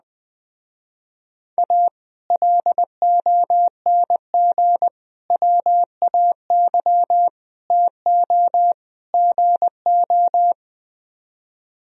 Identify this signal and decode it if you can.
Morse code, heard as a keyed tone, about 21 wpm, 700 Hz. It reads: A LONG WAY TO GO